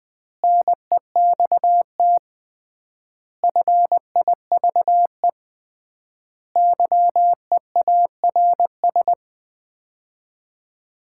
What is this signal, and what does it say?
Morse code, heard as a keyed tone, about 20 wpm, 700 Hz. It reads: NEXT FIVE YEARS
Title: 3x Word-Spacing - Sets of 5 Words - 20wpm